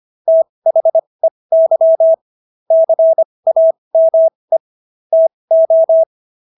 Morse code, heard as a keyed tone, about 25 wpm, spaced as for 20 wpm, 650 Hz. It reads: THEY CAME TO